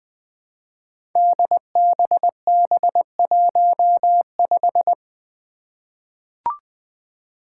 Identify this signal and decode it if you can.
Morse code, heard as a keyed tone, about 20 wpm, 700 Hz. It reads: DBB15
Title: Five Challenging Letter-Number Combinations 20wpm